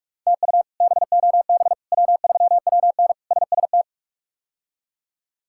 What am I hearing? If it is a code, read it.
Morse code, heard as a keyed tone, about 46 wpm, 700 Hz. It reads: TU BOB W3WN SST
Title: K1USN SST QSO Practice - 45wpm